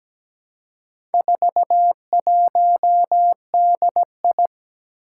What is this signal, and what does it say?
Morse code, heard as a keyed tone, about 17 wpm, 700 Hz. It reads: 41DI
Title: Four Challenging Letter-Number Combinations 17wpm